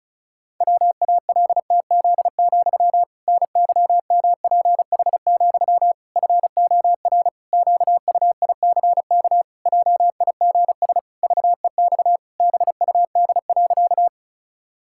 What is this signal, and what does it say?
Morse code, heard as a keyed tone, about 36 wpm, 700 Hz. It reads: WALTZ, NYMPH, FOR QUICK JIGS VEX BUD.
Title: Pangram Challenge v2 - 35wpm